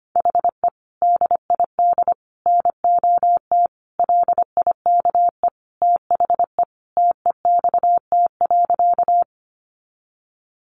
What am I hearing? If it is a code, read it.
Morse code, heard as a keyed tone, about 25 wpm, 700 Hz. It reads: HE DID NOT LIKE THE TEXT.
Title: Sentences from Top 1200 Words 25wpm